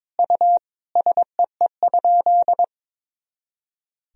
Morse code, heard as a keyed tone, about 22 wpm, 700 Hz. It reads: U SEE?
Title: Sentences from Top 100 Words and Abbreviations 22wpm